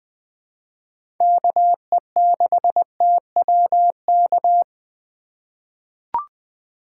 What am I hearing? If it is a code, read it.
Morse code, heard as a keyed tone, about 20 wpm, 700 Hz. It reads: KE6TWK
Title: USA Call Signs - 6 Characters - 4x Character Spacing - 20wpm